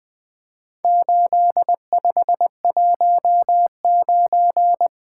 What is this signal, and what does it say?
Morse code, heard as a keyed tone, about 20 wpm, 700 Hz. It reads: 8519
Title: Four Letter-Number Combinations - 20wpm